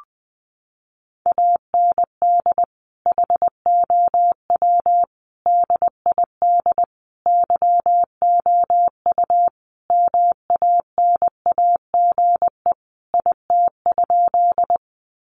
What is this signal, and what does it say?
Morse code, heard as a keyed tone, about 19 wpm, 700 Hz. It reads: AND HOW DID YOU MANAGE IT?
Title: Sentences from Top 1400 Words 20wpm